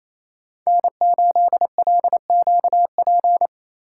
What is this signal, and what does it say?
Morse code, heard as a keyed tone, about 27 wpm, 700 Hz. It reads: N8LQP
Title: USA Call Signs - 5 Characters - 4x Character Spacing - 28wpm